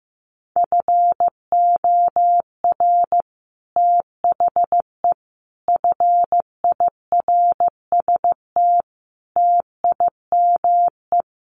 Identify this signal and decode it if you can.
Morse code, heard as a keyed tone, about 15 wpm, 700 Hz. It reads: FOR THE FIRST TIME
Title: Sets of 5 Words 15wpm